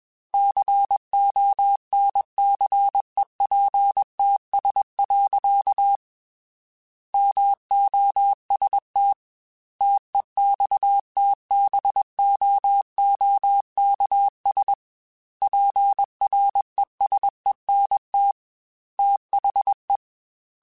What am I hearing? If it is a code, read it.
Morse code, heard as a keyed tone, about 21 wpm, 800 Hz. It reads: CONCEPTS. MOST TEXTBOOKS PRESENT THE
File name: Technical Subjects / Technical Subjects-mp3s / Gates Intro @20